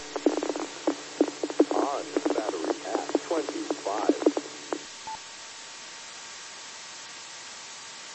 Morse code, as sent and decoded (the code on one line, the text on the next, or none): none